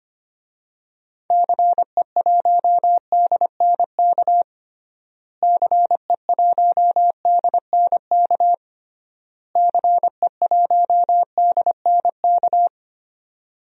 Morse code, heard as a keyed tone, about 25 wpm, 700 Hz. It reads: CE1DNK CE1DNK CE1DNK
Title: DXCC-100 Call Signs - 6 Characters - Repeated 3x - 25wpm